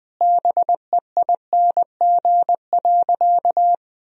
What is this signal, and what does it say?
Morse code, heard as a keyed tone, about 20 wpm, 700 Hz. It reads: BEING.
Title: Sentences from Top 1000 Words - Review Part 3 - 20wpm